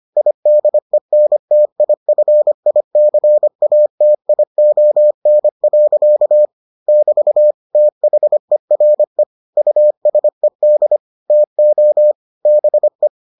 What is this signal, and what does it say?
Morse code, heard as a keyed tone, about 25 wpm, 600 Hz. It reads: IDENTIFICATION. <BT> THERE USED TO BE